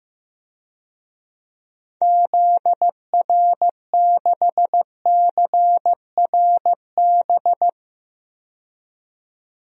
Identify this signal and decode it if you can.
Morse code, heard as a keyed tone, about 16 wpm, 700 Hz. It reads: ZR6CRB
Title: DXCC-100 Call Signs - 6 Characters - 15wpm